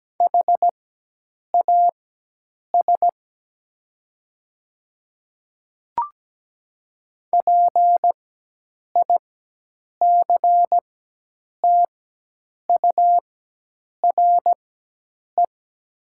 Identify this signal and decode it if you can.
Morse code, heard as a keyed tone, about 17 wpm, spaced as for 8 wpm, 700 Hz. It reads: HAS PICTURE
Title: 4x Character Spacing - Top 200 Words - 17wpm